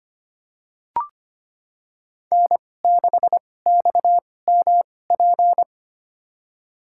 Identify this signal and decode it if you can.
Morse code, heard as a keyed tone, about 25 wpm, spaced as for 18 wpm, 700 Hz. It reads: N6XMP
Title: USA Call Signs - 5 Characters - 2x Character Spacing - 25wpm